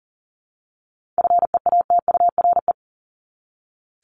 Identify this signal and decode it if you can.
Morse code, heard as a keyed tone, about 40 wpm, 700 Hz. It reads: FEATURE
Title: Top 751-1000 Words 40wpm